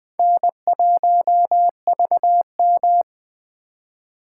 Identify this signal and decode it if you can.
Morse code, heard as a keyed tone, about 20 wpm, 700 Hz. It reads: N1VM
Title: USA Call Signs - 4 Characters - 4x Character Spacing - 20wpm